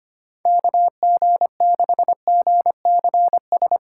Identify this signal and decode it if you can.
Morse code, heard as a keyed tone, about 25 wpm, 700 Hz. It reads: KG6GCS
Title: USA Call Signs - 6 Characters - 2x Character Spacing - 25wpm